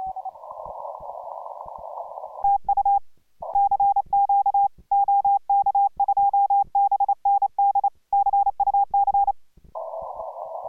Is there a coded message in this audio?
Yes: Morse code from 2.4 to 9.3 s, heard as a keyed tone, about 28 wpm, 810 Hz. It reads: TU CQ OK2BND CUC